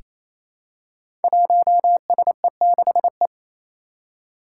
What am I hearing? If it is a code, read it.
Morse code, heard as a keyed tone, about 28 wpm, 700 Hz. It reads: 1SE6E
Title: Five Challenging Letter-Number Combinations 28wpm